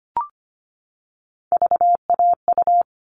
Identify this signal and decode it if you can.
Morse code, heard as a keyed tone, about 25 wpm, 700 Hz. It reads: VAU
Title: All Cut Numbers - 3-Digit 25wpm